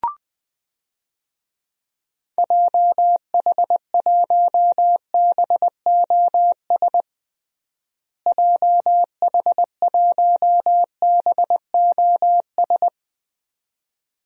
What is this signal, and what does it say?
Morse code, heard as a keyed tone, about 20 wpm, 700 Hz. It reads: JH1BOS JH1BOS
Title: DXCC-100 Call Signs - 6 Characters - Repeated 2x - 20wpm